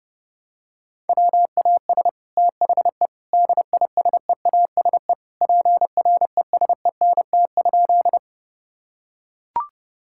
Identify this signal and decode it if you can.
Morse code, heard as a keyed tone, about 30 wpm, 700 Hz. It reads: WAS THE DISEASE PRESENT?
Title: Sentences from Top 1000 Words - Review Part 6 - 30wpm